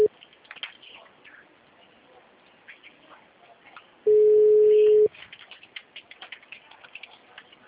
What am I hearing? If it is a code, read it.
Ringback tone (Europe, 425 Hz).